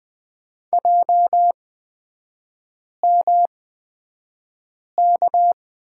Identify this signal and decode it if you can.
Morse code, heard as a keyed tone, about 20 wpm, spaced as for 5 wpm, 700 Hz. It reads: JMK
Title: 224 Lesson 36e - BT - Three Characters 20wpm